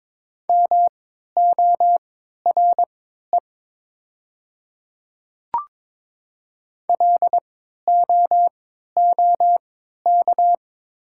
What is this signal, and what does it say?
Morse code, heard as a keyed tone, about 23 wpm, spaced as for 13 wpm, 700 Hz. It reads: MORE LOOK
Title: 3x Character Spacing - Top 100 Words - 22wpm